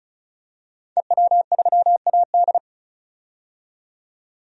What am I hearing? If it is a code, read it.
Morse code, heard as a keyed tone, about 36 wpm, 700 Hz. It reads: EW3AD
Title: DXCC-100 Call Signs - 5 Characters - 35wpm